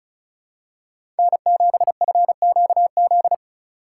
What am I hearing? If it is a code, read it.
Morse code, heard as a keyed tone, about 36 wpm, 700 Hz. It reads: N7FQZ
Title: USA Call Signs - 5 Characters - 4x Character Spacing - 35wpm